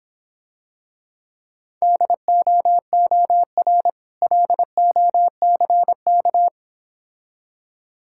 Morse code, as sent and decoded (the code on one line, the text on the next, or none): -.. --- --- .-. / .-.. --- -.-. -.-
DOOR LOCK